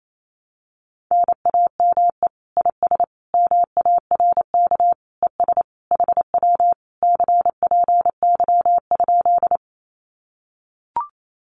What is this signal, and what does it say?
Morse code, heard as a keyed tone, about 28 wpm, 700 Hz. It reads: NAME IS MARK ES HW CPY?
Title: Common QSO Phrases Encore 28wpm